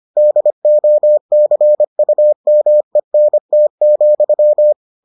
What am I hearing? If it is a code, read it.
Morse code, heard as a keyed tone, about 25 wpm, 600 Hz. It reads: DOCUMENT,